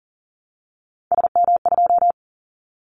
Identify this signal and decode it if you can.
Morse code, heard as a keyed tone, about 40 wpm, 700 Hz. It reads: SM2
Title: Callsign Prefixes for Top 25 DXCC - Repeated 3x - 40wpm